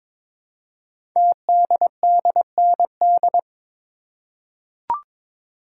Morse code, heard as a keyed tone, about 22 wpm, 700 Hz. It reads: TDDND
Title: Code Groups - 5 Letters - 22wpm